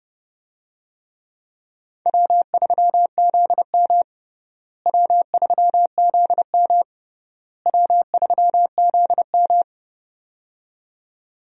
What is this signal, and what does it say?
Morse code, heard as a keyed tone, about 30 wpm, 700 Hz. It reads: W3ZM W3ZM W3ZM
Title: USA Call Signs - 4 Characters - Repeated 3x - 30wpm